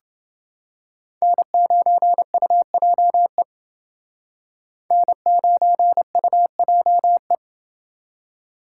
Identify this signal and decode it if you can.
Morse code, heard as a keyed tone, about 28 wpm, 700 Hz. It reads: N9UJE N9UJE